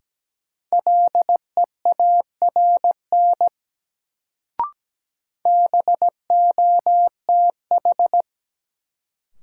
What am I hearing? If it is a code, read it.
Morse code, heard as a keyed tone, about 17 wpm, 700 Hz. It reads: LEARN BOTH
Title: Sentences from Top 300 Words - New Word Drill - 17wpm